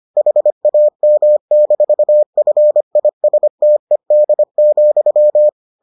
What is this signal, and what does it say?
Morse code, heard as a keyed tone, about 25 wpm, 600 Hz. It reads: HAM-FISTED,